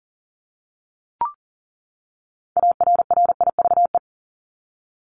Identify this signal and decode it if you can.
Morse code, heard as a keyed tone, about 40 wpm, 700 Hz. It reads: ARRIVE
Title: ED Suffix - Top 100 Words with Base Words 40wpm